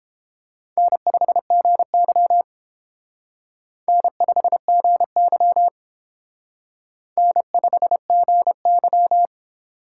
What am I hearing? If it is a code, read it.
Morse code, heard as a keyed tone, about 30 wpm, 700 Hz. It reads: N5GY N5GY N5GY